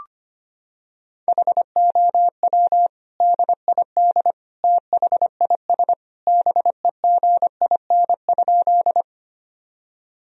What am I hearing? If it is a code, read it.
Morse code, heard as a keyed tone, about 25 wpm, 700 Hz. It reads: HOW DID THIS BEGIN?
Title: Sentences from Top 1000 Words - Review Part 1 - 25wpm